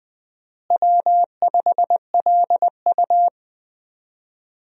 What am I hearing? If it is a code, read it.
Morse code, heard as a keyed tone, about 21 wpm, 700 Hz. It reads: W5LU
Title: USA Call Signs - 4 Characters - Repeated 3x - 20wpm